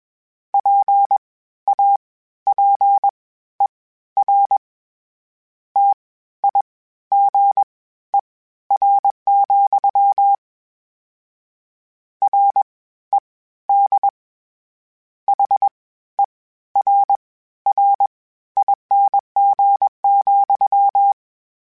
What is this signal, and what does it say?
Morse code, heard as a keyed tone, about 21 wpm, spaced as for 13 wpm, 800 Hz. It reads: PAPER TIGER, RED HERRING,
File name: Misc / Misc-mp3s / WordPairs sp @20